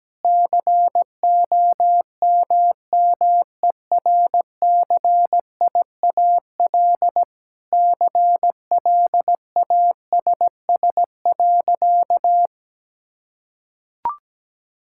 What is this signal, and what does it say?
Morse code, heard as a keyed tone, about 17 wpm, 700 Hz. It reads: COMMERCIAL CLASS.
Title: Sentences from Top 1000 Words - Review Part 5 - 17wpm